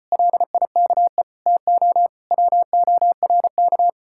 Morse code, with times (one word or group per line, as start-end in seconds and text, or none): 0.1-1.2 LIKE
1.5-2.1 TO
2.3-3.9 WORK